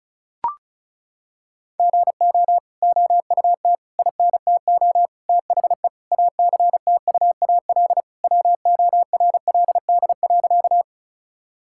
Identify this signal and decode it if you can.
Morse code, heard as a keyed tone, about 36 wpm, 700 Hz. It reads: GO OUT INTO THE ACTUAL WORLD.
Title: Sentences from Top 1000 Words - Review Part 5 - 35wpm